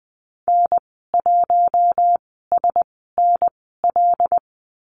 Morse code, heard as a keyed tone, about 20 wpm, spaced as for 14 wpm, 700 Hz. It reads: N1SNL
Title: 082 Lesson 14g - Number 2 - Call Signs 20wpm